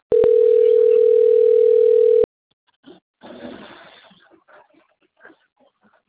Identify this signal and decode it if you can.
Ringback tone (North America), 440 + 480 Hz.